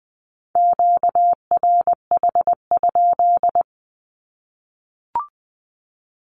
Morse code, heard as a keyed tone, about 20 wpm, 700 Hz. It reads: QRH?
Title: Q-Codes 20wpm